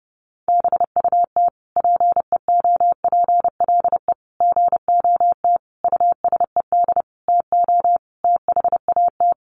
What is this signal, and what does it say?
Morse code, heard as a keyed tone, about 30 wpm, 700 Hz. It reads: BUT PEOPLE GOT USED TO THAT
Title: Sentences from Top 700 Words 30wpm